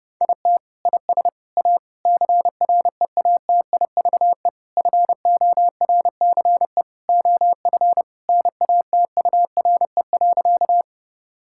Morse code, heard as a keyed tone, about 32 wpm, 700 Hz. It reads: IT IS A CREATIVE FORCE OF NATURE.
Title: Sentences from Top 1000 Words - Review Part 5 - 30wpm